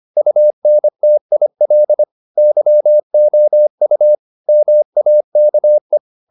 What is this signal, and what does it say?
Morse code, heard as a keyed tone, about 25 wpm, 600 Hz. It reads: UNTIL YOU MAKE
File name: the old man and the sea part 1 25wpm